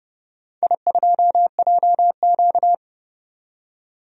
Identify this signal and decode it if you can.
Morse code, heard as a keyed tone, about 30 wpm, 700 Hz. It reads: I2JQ
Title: DXCC-100 Call Signs - 4 Characters - 30wpm